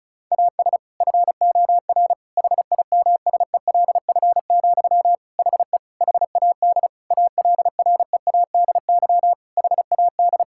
Morse code, heard as a keyed tone, about 36 wpm, 700 Hz. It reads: AS FOR HIMSELF, HE HAD ALREADY HAD
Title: Sentences from Top 1600 Words 35wpm